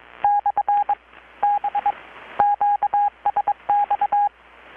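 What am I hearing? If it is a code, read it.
Morse code, heard as a keyed tone, about 22 wpm, spaced as for 17 wpm, 820 Hz, with radio noise. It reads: /B QSX
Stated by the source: Morse from KPH's Night of Nights XX maritime radiotelegraphy event, received in Chiapas, Mexico about 3,800 km away on a Kenwood R-600 with a grounded 10 m long-wire antenna.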